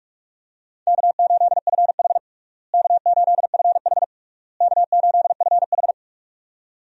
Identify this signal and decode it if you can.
Morse code, heard as a keyed tone, about 46 wpm, 700 Hz. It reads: K8FH K8FH K8FH
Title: USA Call Signs - 4 Characters - Repeated 3x - 45wpm